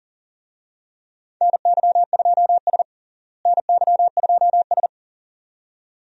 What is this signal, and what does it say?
Morse code, heard as a keyed tone, about 40 wpm, 700 Hz. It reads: NY2S NY2S
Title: USA Call Signs - 4 Characters - Repeated 2x - 40wpm